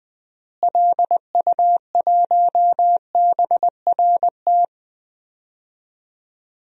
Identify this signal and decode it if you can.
Morse code, heard as a keyed tone, about 20 wpm, 700 Hz. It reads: LU1BRT
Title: DXCC-100 Call Signs - 6 Characters - Repeated 3x - 20wpm